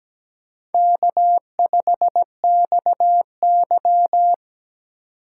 Morse code, heard as a keyed tone, about 17 wpm, 700 Hz. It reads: K5XY
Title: USA Call Signs - 4 Characters - 4x Character Spacing - 17wpm